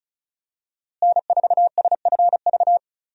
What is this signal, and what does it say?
Morse code, heard as a keyed tone, about 36 wpm, 700 Hz. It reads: N4SFV
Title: USA Call Signs - 5 Characters - Repeated 2x - 35wpm